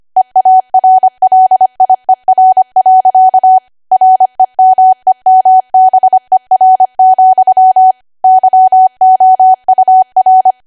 Morse code, heard as a keyed tone, about 25 wpm, 750 Hz. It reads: EARLIER. REMEMBER, YOUR
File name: W1AW-Level3/140218 25WPM